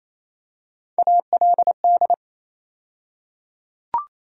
Morse code, heard as a keyed tone, about 28 wpm, 700 Hz. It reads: ALD
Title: USA Call Sign Postfixes - 3 Letters 28wpm